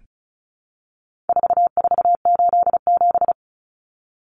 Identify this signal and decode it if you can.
Morse code, heard as a keyed tone, about 34 wpm, 700 Hz. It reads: 4487
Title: Numbers - 4-Digit 35wpm